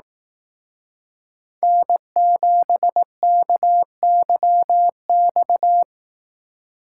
Morse code, heard as a keyed tone, about 18 wpm, 700 Hz. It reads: N7KYX
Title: Speed Racing - USA Call Signs 15wpm